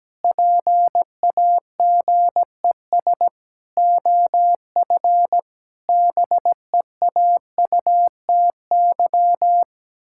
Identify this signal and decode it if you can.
Morse code, heard as a keyed tone, about 17 wpm, 700 Hz. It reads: PAGES OF BEAUTY
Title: Pangram Challenge v2 - 17wpm